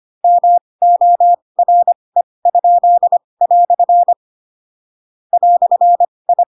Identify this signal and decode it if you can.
Morse code, heard as a keyed tone, about 25 wpm, spaced as for 20 wpm, 700 Hz. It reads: MORE?" "I
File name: Pinocchio / Pinocchio ch8 15 25 ik1hge